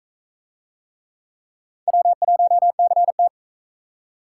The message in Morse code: W1CT

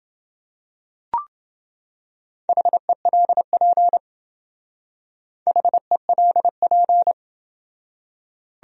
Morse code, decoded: HELP HELP

